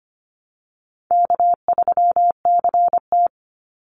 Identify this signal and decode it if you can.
Morse code, heard as a keyed tone, about 25 wpm, 700 Hz. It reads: K3CT
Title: Easy Call Signs Repeated 3x - 25wpm